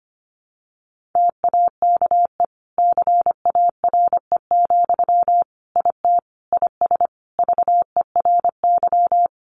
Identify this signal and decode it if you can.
Morse code, heard as a keyed tone, about 25 wpm, 700 Hz. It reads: TAKE CARE, IT IS VERY
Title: Sentences from Top 1100 Words 25wpm